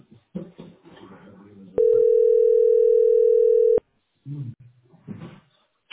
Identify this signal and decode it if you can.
Ringback tone (North America), 440 + 480 Hz.